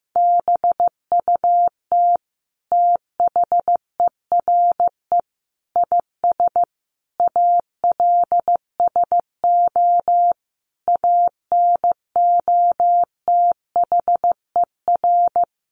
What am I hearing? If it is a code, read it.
Morse code, heard as a keyed tone, about 14 wpm, 700 Hz. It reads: BUT THERE IS ALSO ANOTHER
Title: Sentences from Top 1500 Words 15wpm